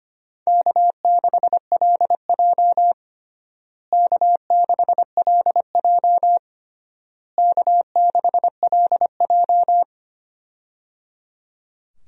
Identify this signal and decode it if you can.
Morse code, heard as a keyed tone, about 25 wpm, 700 Hz. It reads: K6LJ K6LJ K6LJ
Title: USA Call Signs - 4 Characters - Repeated 3x - 25wpm